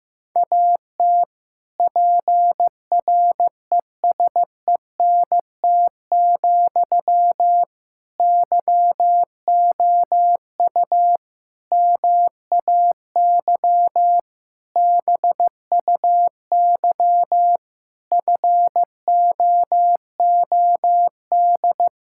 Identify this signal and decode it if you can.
Morse code, heard as a keyed tone, about 15 wpm, 700 Hz. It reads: AT PRESENT, YOU MAY BUY FOOD
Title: Sentences from Top 1000 Words - Review Part 5 - 15wpm